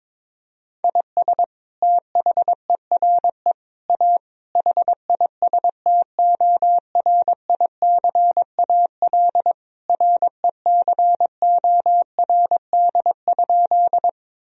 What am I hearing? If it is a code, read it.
Morse code, heard as a keyed tone, about 22 wpm, 700 Hz. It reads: IS THERE A HISTORICAL RECORD?
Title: Sentences from Top 1000 Words - Review Part 4 - 22wpm